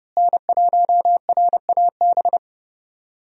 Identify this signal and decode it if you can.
Morse code, heard as a keyed tone, about 30 wpm, 700 Hz. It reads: N1RAB
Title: USA Call Signs - 5 Characters - Repeated 2x - 30wpm